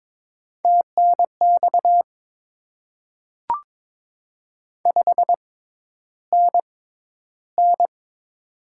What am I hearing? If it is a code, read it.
Morse code, heard as a keyed tone, about 22 wpm, spaced as for 11 wpm, 700 Hz. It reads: TNX 5NN
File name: Top 20 QSO Elements - 6x Character Spacing - 22wpm